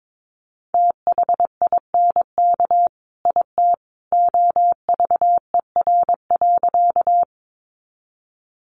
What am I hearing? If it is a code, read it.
Morse code, heard as a keyed tone, about 22 wpm, 700 Hz. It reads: THINK IT OVER.
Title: Sentences from Top 200 Words 22wpm